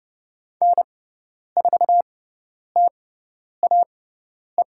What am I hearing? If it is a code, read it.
Morse code, heard as a keyed tone, about 30 wpm, spaced as for 10 wpm, 700 Hz. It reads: N4TAE
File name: Lesson 16g - Letter c - Call Signs 30wpm